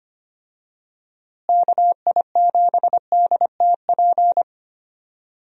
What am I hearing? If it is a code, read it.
Morse code, heard as a keyed tone, about 25 wpm, 700 Hz. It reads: KI7DTP
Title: USA Call Signs - 6 Characters - Repeated 3x - 25wpm